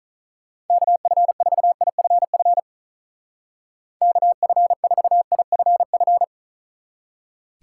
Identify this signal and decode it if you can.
Morse code, heard as a keyed tone, about 45 wpm, 700 Hz. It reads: KF4IFF KF4IFF